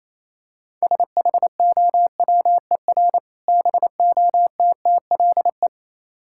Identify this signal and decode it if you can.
Morse code, heard as a keyed tone, about 28 wpm, 700 Hz. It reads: SHOWER BOTTLE